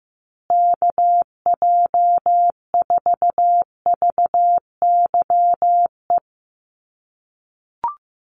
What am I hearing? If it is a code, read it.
Morse code, heard as a keyed tone, about 15 wpm, 700 Hz. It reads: KJ4VYE
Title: USA Call Signs Repeated 2x 15wpm